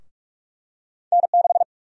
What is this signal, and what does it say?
Morse code, heard as a keyed tone, about 46 wpm, 700 Hz. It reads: N6
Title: USA Call Sign Prefixes - 45wpm